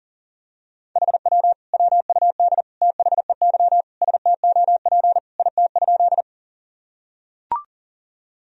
Morse code, heard as a keyed tone, about 40 wpm, 700 Hz. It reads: HW WUD THEY STOP IT?
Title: Sentences from Top 300 Words and Abbreviations 40wpm